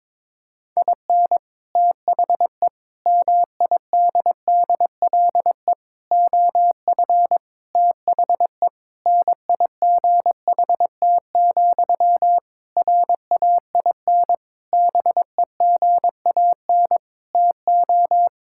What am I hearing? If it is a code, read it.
Morse code, heard as a keyed tone, about 21 wpm, 700 Hz. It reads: IN THE MIDDLE OF THE NIGHT, RAIN BEGAN TO
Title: Sentences from Top 1000 Words - Review Part 6 - 22wpm